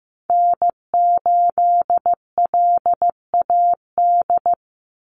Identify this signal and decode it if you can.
Morse code, heard as a keyed tone, about 15 wpm, 700 Hz. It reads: N8LAD
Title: USA Call Signs 15wpm